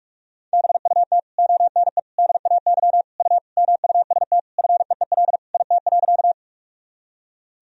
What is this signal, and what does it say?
Morse code, heard as a keyed tone, about 46 wpm, 700 Hz. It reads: BUT ONE DAY U MUST FEEL IT.
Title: Sentences from Top 400 Words and Abbreviations 45wpm